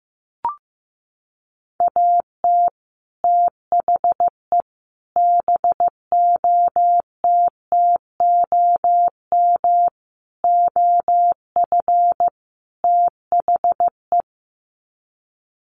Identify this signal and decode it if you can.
Morse code, heard as a keyed tone, about 15 wpm, 700 Hz. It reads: AT THE BOTTOM OF THE
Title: Sets of 5 Words 15wpm